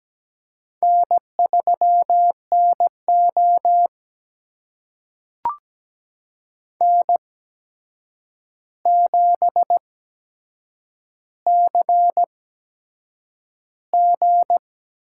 Morse code, heard as a keyed tone, about 17 wpm, spaced as for 7 wpm, 700 Hz. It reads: N3NO N7CG
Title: USA Call Signs - 4 Characters - 8x Character Spacing - 17wpm